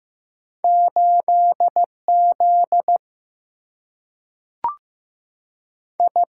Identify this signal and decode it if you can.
Morse code, heard as a keyed tone, about 15 wpm, 700 Hz. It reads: 8 Z I
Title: Single-Letter Single-Number 15wpm